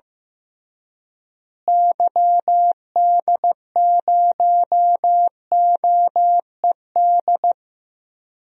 Morse code, heard as a keyed tone, about 15 wpm, 700 Hz. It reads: YD0OED
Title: DXCC-100 Call Signs - 6 Characters - Repeated 3x - 15wpm